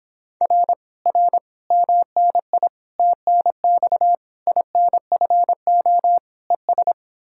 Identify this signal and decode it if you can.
Morse code, heard as a keyed tone, about 29 wpm, 700 Hz. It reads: R R MNI TNX INFO ES